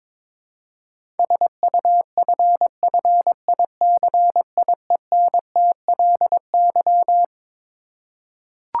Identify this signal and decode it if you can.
Morse code, heard as a keyed tone, about 21 wpm, 700 Hz. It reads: SUFFICIENTLY
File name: Twelve Letter Words - 22wpm